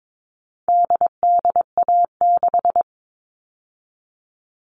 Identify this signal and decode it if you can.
Morse code, heard as a keyed tone, about 22 wpm, 700 Hz. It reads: DDA6
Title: All Cut Numbers - 4-Digit 22wpm